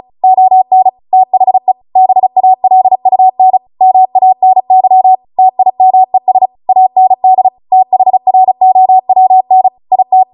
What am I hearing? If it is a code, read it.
Morse code, heard as a keyed tone, about 36 wpm, 750 Hz. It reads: ON THE BALUN MANY TIMES AND THROWN IT